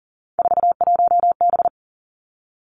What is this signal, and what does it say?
Morse code, heard as a keyed tone, about 40 wpm, 700 Hz. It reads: 41B
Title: Three Challenging Letter-Number Combinations 40wpm